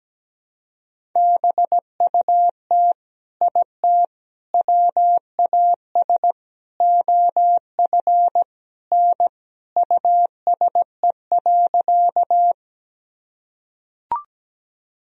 Morse code, heard as a keyed tone, about 17 wpm, 700 Hz. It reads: BUT IT WAS OF N USE.
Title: Sentences from Top 100 Words and Abbreviations 17wpm